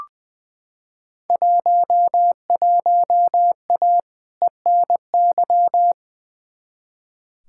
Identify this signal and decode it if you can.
Morse code, heard as a keyed tone, about 20 wpm, 700 Hz. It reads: 11A ENY
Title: Field Day Exchanges Encore - 20wpm